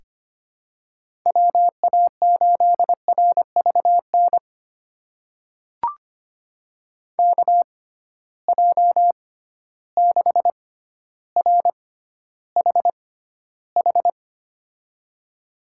Morse code, heard as a keyed tone, about 25 wpm, spaced as for 13 wpm, 700 Hz. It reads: WA8RVN KJ6RHH
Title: USA Call Signs - 6 Characters - 6x Character Spacing - 25wpm